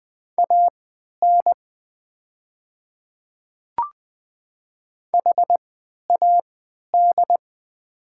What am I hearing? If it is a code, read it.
Morse code, heard as a keyed tone, about 21 wpm, spaced as for 12 wpm, 700 Hz. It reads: AN HAD
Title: 3x Character Spacing - Top 100 Words - 20wpm